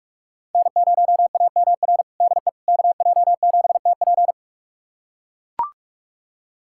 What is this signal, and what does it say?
Morse code, heard as a keyed tone, about 41 wpm, 700 Hz. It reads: N0AMR DE KJ7TP
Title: USA Callsign DE Callsign - 5 Characters - 45wpm